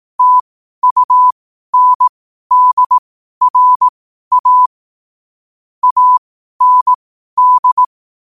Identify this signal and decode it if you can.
Morse code, heard as a keyed tone, about 18 wpm, spaced as for 12 wpm, 1000 Hz. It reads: TUNDRA AND
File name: qotdfiles / 12wpm